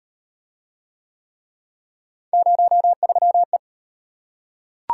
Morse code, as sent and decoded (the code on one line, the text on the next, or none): ----- ...-- .
03E